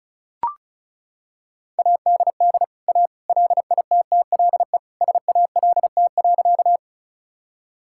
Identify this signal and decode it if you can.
Morse code, heard as a keyed tone, about 34 wpm, 700 Hz. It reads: ADD A LITTLE SALT.
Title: Sentences from Top 1000 Words - Review Part 5 - 35wpm